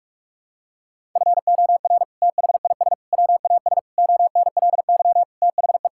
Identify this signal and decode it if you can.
Morse code, heard as a keyed tone, about 46 wpm, 700 Hz. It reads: FOR THIS WAS ONLY THE
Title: Sentences from Top 1000 Words - Review Part 5 - 45wpm